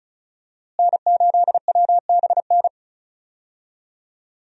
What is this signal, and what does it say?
Morse code, heard as a keyed tone, about 34 wpm, 700 Hz. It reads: N8WBN
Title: USA Call Signs - 5 Characters - Repeated 2x - 35wpm